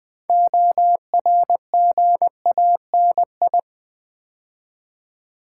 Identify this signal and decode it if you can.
Morse code, heard as a keyed tone, about 20 wpm, 700 Hz. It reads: ORGANI
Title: Build A Word - 3 Letters At A Time - 9-Letter Words - 20wpm